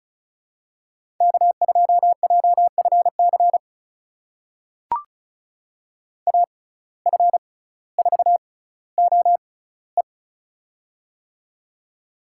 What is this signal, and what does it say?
Morse code, heard as a keyed tone, about 34 wpm, spaced as for 18 wpm, 700 Hz. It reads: K2JFC AF4OE
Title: USA Call Signs - 5 Characters - 6x Character Spacing - 35wpm